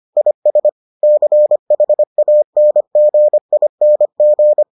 Morse code, heard as a keyed tone, about 25 wpm, 600 Hz. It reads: IS CHANGING